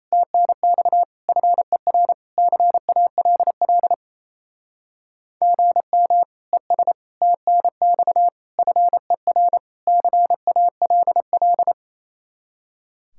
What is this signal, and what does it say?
Morse code, heard as a keyed tone, about 38 wpm, 700 Hz. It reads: TNX FER CALL GM ES TNX FER CALL